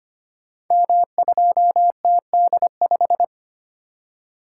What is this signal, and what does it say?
Morse code, heard as a keyed tone, about 25 wpm, 700 Hz. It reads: M2TD5
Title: Code Groups - 5 Letter:Numbers - 25wpm